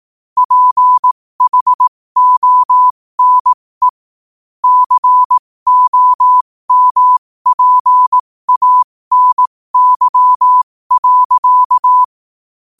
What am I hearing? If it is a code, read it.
Morse code, heard as a keyed tone, about 18 wpm, spaced as for 15 wpm, 1000 Hz. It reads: PHONE COMPANY.
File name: qotdfiles / 15wpm